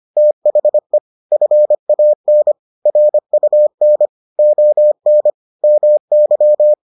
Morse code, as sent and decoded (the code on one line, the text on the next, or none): - .... / ..-. .- -. / .-. ..- -. / --- -. / -- -.--
THE FAN RUN ON MY